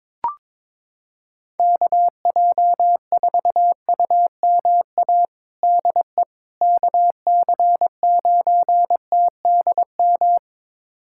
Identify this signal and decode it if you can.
Morse code, heard as a keyed tone, about 22 wpm, 700 Hz. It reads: KJ4UMA DE KC9TDM
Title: USA Callsign DE Callsign - 6 Characters - 22wpm